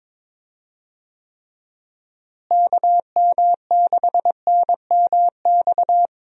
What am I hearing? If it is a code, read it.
Morse code, heard as a keyed tone, about 22 wpm, 700 Hz. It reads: KM6NMX
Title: USA Call Signs - 6 Characters - Repeated 2x - 22wpm